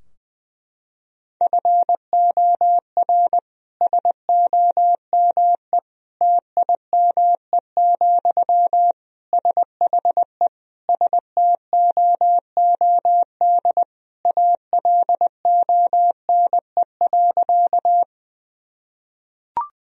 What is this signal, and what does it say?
Morse code, heard as a keyed tone, about 21 wpm, 700 Hz. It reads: FOR SOME TIME, SHE STOOD ALONE.
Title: Sentences from Top 1000 Words - Review Part 2 - 20wpm